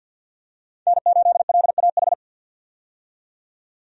Morse code, heard as a keyed tone, about 48 wpm, 700 Hz. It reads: N8LAH
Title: USA Call Signs - 5 Characters - 50wpm